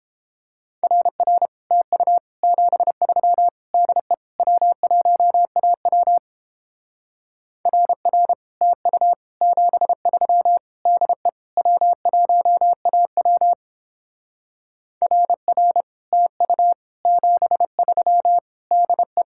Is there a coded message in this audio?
Yes: Morse code from 0.8 to 19.2 s, heard as a keyed tone, about 29 wpm, 700 Hz. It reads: RR TU 73 DE W1AW RR TU 73 DE W1AW RR TU 73 DE